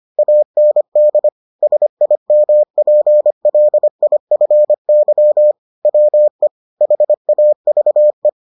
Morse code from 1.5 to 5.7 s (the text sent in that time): SIMPLIFY